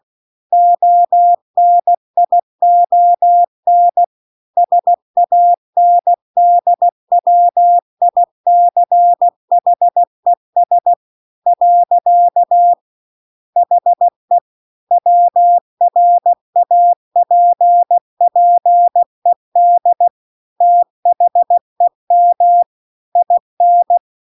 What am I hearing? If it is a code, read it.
Morse code, heard as a keyed tone, about 16 wpm, 700 Hz. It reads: ONION SANDWICHES . HE WRAPPED THEM IN
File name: CW-The-Big-Twoheaded-River-2-16WPM